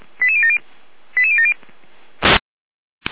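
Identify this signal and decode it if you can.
Five-tone selective call (ZVEI), heard as quick runs of five tones, tones of about 70 ms. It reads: 88070 88070
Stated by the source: ZVEI five-tone selective call received on an SDR.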